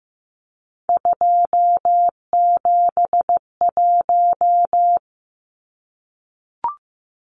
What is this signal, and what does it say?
Morse code, heard as a keyed tone, about 15 wpm, 700 Hz. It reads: 271
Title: Numbers - 3-Digit 15wpm